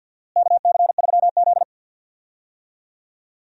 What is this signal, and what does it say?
Morse code, heard as a keyed tone, about 51 wpm, 700 Hz. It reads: KC3Z
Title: USA Call Signs - 4 Characters - Repeated 3x - 50wpm